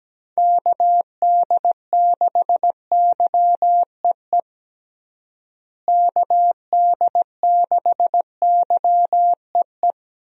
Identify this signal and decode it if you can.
Morse code, heard as a keyed tone, about 17 wpm, 700 Hz. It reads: KD6YEE KD6YEE
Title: USA Call Signs - 6 Characters - Repeated 3x - 17wpm